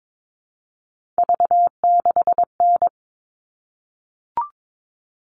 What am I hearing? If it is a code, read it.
Morse code, heard as a keyed tone, about 22 wpm, 700 Hz. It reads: V6N
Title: All Cut Numbers - 3-Digit 22wpm